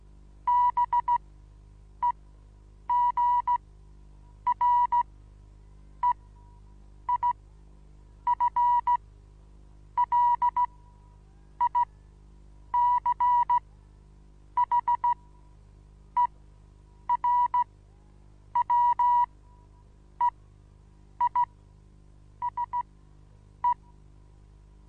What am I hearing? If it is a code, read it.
Morse code, heard as a keyed tone, about 16 wpm, spaced as for 7 wpm, 970 Hz. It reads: BEGREIFLICHERWEISE
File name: Onlinelehrgang / morsen / DJ4UF Morsen25